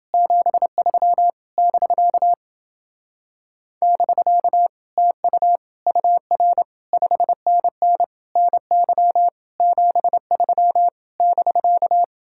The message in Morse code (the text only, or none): 73 <BK> <BK> TU UR 5NN NY 73 <BK>